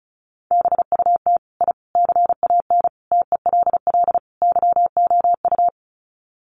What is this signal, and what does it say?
Morse code, heard as a keyed tone, about 34 wpm, 700 Hz. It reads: BUT I CAN TELL YOU
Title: Sets of 5 Words Encore 35wpm